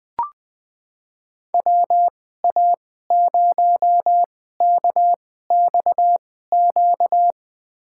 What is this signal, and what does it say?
Morse code, heard as a keyed tone, about 21 wpm, spaced as for 15 wpm, 700 Hz. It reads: WA0KXQ
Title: USA Call Signs - 6 Characters - 2x Character Spacing - 20wpm